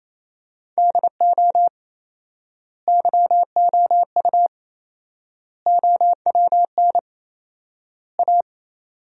Morse code, heard as a keyed tone, about 28 wpm, 700 Hz. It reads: DO YOU OWN A
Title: 4x Word-Spacing - Sentences from Top 400 Words - 28wpm